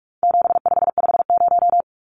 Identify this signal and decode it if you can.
Morse code, heard as a keyed tone, about 44 wpm, 700 Hz. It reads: 7550
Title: Numbers - 4-Digit 45wpm